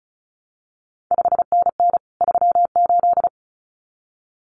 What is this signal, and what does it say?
Morse code, heard as a keyed tone, about 36 wpm, 700 Hz. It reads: 5NN 38
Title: CQ WW DX Contest Exchange Encore - 35wpm